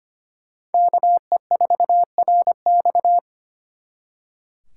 Morse code, heard as a keyed tone, about 32 wpm, 700 Hz. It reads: KE4RX